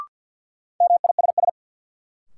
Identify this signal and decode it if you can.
Morse code, heard as a keyed tone, about 50 wpm, 700 Hz. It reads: MISS
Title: Sentences from Top 500 Words - New Word Drill - 50wpm